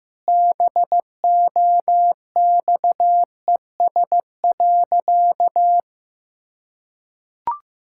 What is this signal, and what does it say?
Morse code, heard as a keyed tone, about 15 wpm, 700 Hz. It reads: BOXES.
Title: Pangram Challenge v2 - 15wpm